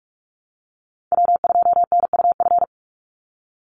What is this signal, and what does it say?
Morse code, heard as a keyed tone, about 44 wpm, 700 Hz. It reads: W2NUF
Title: USA Call Signs 45wpm